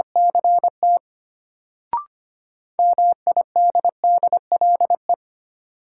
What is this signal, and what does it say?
Morse code, heard as a keyed tone, about 25 wpm, 700 Hz. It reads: CT MIDDLE
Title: Sentences from Top 900 Words - New Word Drill - 25wpm